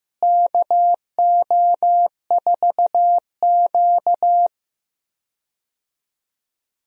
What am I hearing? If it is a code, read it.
Morse code, heard as a keyed tone, about 15 wpm, 700 Hz. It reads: KO4Q
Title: USA Call Signs - 4 Characters - Repeated 3x - 15wpm